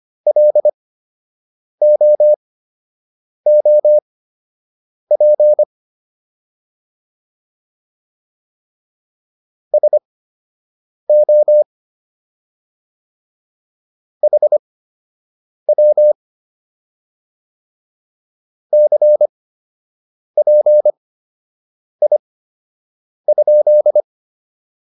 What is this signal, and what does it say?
Morse code, heard as a keyed tone, about 25 wpm, spaced as for 7 wpm, 600 Hz. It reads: LOOP SO HW CPI?